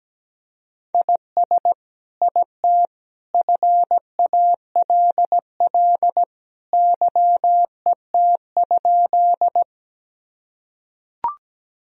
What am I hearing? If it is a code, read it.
Morse code, heard as a keyed tone, about 17 wpm, 700 Hz. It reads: IS IT FALL YET?
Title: Sentences from Top 1000 Words - Review Part 2 - 17wpm